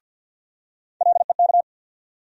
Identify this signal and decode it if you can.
Morse code, heard as a keyed tone, about 46 wpm, 700 Hz. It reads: LEK
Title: USA Call Sign Postfixes - 3 Letters 50wpm